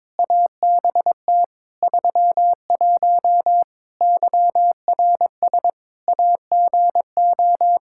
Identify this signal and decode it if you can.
Morse code, heard as a keyed tone, about 21 wpm, 700 Hz. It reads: ABT 31 YRS AGO
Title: Practice QSO - Exchange 3 - More Info - 22wpm